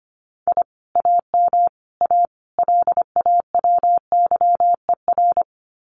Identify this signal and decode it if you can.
Morse code, heard as a keyed tone, about 25 wpm, 700 Hz. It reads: I AM A LAWYER
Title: Common QSO Phrases Encore 25wpm